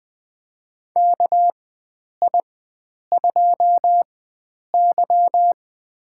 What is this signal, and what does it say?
Morse code, heard as a keyed tone, about 20 wpm, spaced as for 9 wpm, 700 Hz. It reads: KI2Y
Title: USA Call Signs - 4x Character Spacing - 20wpm